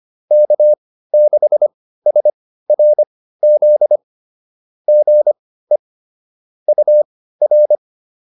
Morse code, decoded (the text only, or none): K6SRZ GE UR